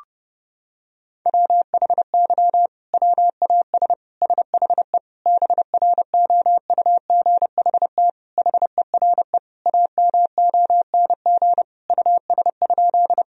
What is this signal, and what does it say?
Morse code, heard as a keyed tone, about 30 wpm, 700 Hz. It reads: WHY WAS SHE BROUGHT HERE AMONG US?
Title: Sentences from Top 1000 Words - Review Part 3 - 30wpm